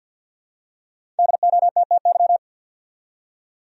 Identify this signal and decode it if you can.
Morse code, heard as a keyed tone, about 47 wpm, 700 Hz. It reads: DOTTY
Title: USA Common Names - Max 5 Letters - 50wpm